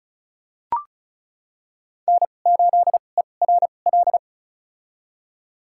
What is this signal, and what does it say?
Morse code, heard as a keyed tone, about 34 wpm, spaced as for 25 wpm, 700 Hz. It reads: N8ERL
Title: USA Call Signs - 5 Characters - 2x Character Spacing - 35wpm